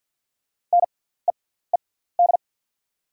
Morse code, heard as a keyed tone, about 48 wpm, spaced as for 17 wpm, 690 Hz. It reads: NEED
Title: 6x Character Spacing - Top 500 Words - 50wpm